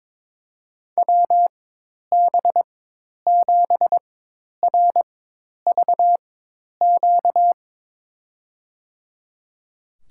Morse code, heard as a keyed tone, about 22 wpm, spaced as for 10 wpm, 700 Hz. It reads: WB7RVQ
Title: USA Call Signs - 6 Characters - 4x Character Spacing - 22wpm